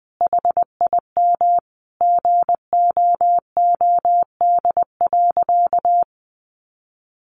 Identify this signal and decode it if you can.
Morse code, heard as a keyed tone, about 20 wpm, 700 Hz. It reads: HIM GOOD.
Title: Sentences from Top 1200 Words 20wpm